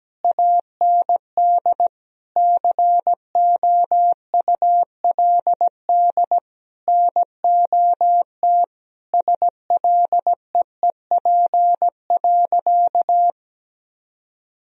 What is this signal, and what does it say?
Morse code, heard as a keyed tone, about 17 wpm, 700 Hz. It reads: AND COULD NOT SLEEP.
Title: Sentences from Top 1000 Words - Review Part 3 - 17wpm